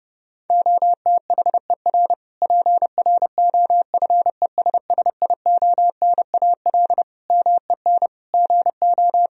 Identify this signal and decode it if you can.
Morse code, heard as a keyed tone, about 30 wpm, 700 Hz. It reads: OTHER PROFESSIONAL MEN GO
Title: Sentences from Top 1000 Words - Review Part 4 - 30wpm